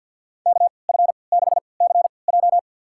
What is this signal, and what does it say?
Morse code, heard as a keyed tone, about 52 wpm, spaced as for 28 wpm, 700 Hz. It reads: KF6CJ
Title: USA Call Signs - 3x Character Spacing - 50wpm